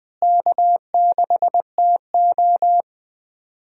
Morse code, heard as a keyed tone, about 21 wpm, 700 Hz. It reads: K6TO
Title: USA Call Signs - 4 Characters - Repeated 2x - 20wpm